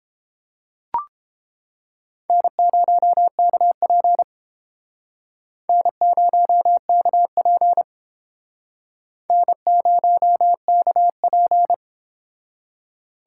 Morse code, heard as a keyed tone, about 28 wpm, 700 Hz. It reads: N0KP N0KP N0KP